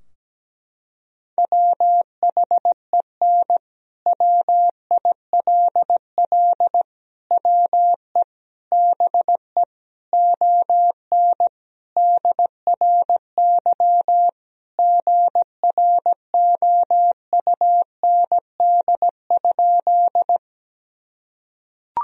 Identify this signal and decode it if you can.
Morse code, heard as a keyed tone, about 17 wpm, 700 Hz. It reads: WHEN WILL WE BE ON DRY GROUND?
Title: Sentences from Top 1000 Words - Review Part 3 - 17wpm